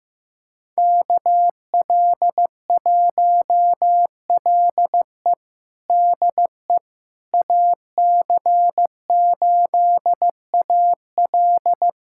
Morse code, heard as a keyed tone, about 15 wpm, 700 Hz. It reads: KL1LE DE AC8AL
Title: USA Callsign DE Callsign - 5 Characters - 15wpm